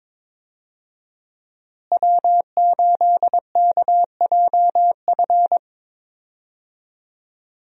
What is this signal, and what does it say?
Morse code, heard as a keyed tone, about 22 wpm, 700 Hz. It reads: W8KJF
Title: USA Call Signs - 5 Characters - Repeated 3x - 22wpm